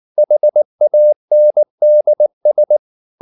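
Morse code, heard as a keyed tone, about 19 wpm, 600 Hz. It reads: HANDS
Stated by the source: Morse code at 19 wpm from CW med Gnister, reading Ken Follett's novel Never in English.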